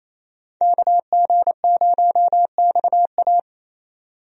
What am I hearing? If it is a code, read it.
Morse code, heard as a keyed tone, about 28 wpm, 700 Hz. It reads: KG0XA
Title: USA Call Signs - 5 Characters - 28wpm